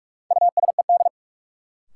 Morse code, heard as a keyed tone, about 46 wpm, 700 Hz. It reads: USED
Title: Sentences from Top 600 Words - New Word Drill - 45wpm